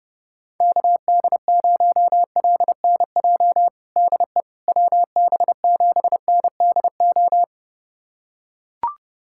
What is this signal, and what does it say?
Morse code, heard as a keyed tone, about 30 wpm, 700 Hz. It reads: KD0LNJ DE WB7NDO
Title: USA Callsign DE Callsign 30wpm